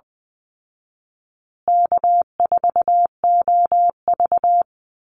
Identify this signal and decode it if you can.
Morse code, heard as a keyed tone, about 20 wpm, 700 Hz. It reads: K4OV
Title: Call Signs Repeated 2x - 20wpm